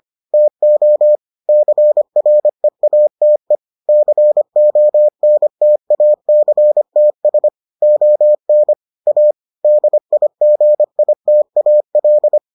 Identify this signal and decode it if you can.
Morse code, heard as a keyed tone, about 25 wpm, 600 Hz. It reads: TO CREATE CONTACTS ON A DIGITAL